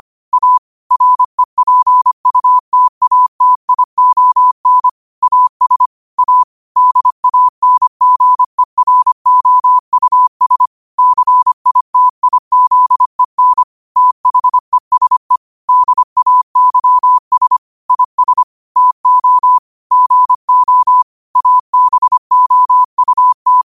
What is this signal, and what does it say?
Morse code, heard as a keyed tone, about 25 wpm, 1000 Hz. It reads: A REPUTATION AS A DANGEROUS CITIZEN THESE DAYS IS TO GO ABOUT
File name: qotdfiles / 25wpm